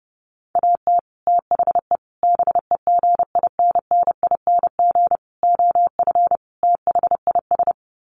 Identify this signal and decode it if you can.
Morse code, heard as a keyed tone, about 30 wpm, 700 Hz. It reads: AT THE BEGINNING OF THIS
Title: Sets of 5 Words Encore 30wpm